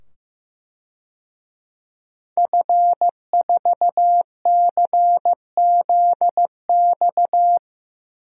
Morse code, heard as a keyed tone, about 15 wpm, 700 Hz. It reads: F4CZX
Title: DXCC-100 Call Signs - 5 Characters - 15wpm